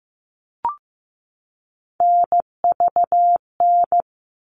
Morse code, heard as a keyed tone, about 15 wpm, 700 Hz. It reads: NVN
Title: All Cut Numbers - 3-Digit 15wpm